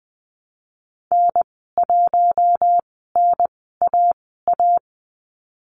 Morse code, heard as a keyed tone, about 20 wpm, spaced as for 14 wpm, 700 Hz. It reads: N1NAA
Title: 042 Lesson 8g - Number 1 - Call Signs 20wpm